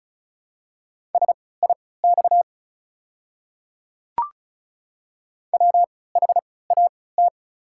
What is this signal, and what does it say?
Morse code, heard as a keyed tone, about 34 wpm, spaced as for 20 wpm, 700 Hz. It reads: SIX WHAT
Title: 3x Character Spacing - Top 500 Words - 35wpm